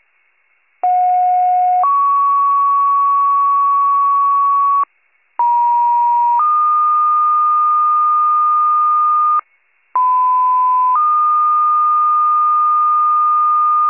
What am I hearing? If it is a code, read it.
Two-tone page, heard as a one-second tone, then a three-second tone, about 730 then 1120 Hz, about 950 then 1220 Hz, about 1010 then 1220 Hz. It is heard from 0.8 to 13.9 s.